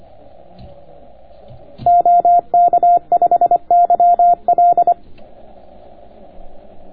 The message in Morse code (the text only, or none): OK5YL